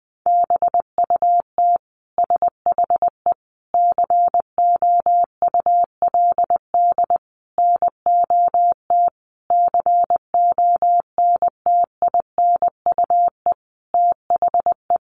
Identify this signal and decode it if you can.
Morse code, heard as a keyed tone, about 20 wpm, 700 Hz. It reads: BUT SHE COULD NOT CONTINUE THE
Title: Sentences from Top 1200 Words 20wpm